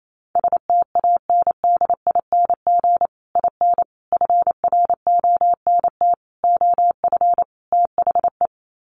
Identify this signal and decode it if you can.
Morse code, heard as a keyed tone, about 28 wpm, 700 Hz. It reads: STANDING IN FRONT OF THE
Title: Sets of 5 Words Encore2 28wpm